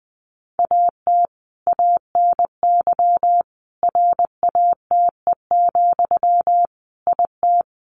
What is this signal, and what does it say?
Morse code, heard as a keyed tone, about 20 wpm, 700 Hz. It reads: AT ANY RATE, IT
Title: Sentences from Top 1200 Words 20wpm